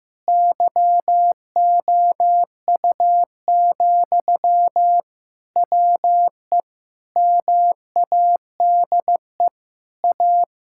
Morse code, heard as a keyed tone, about 15 wpm, 700 Hz. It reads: YOU, WE MADE A
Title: Sentences from Top 1000 Words - Review Part 5 - 15wpm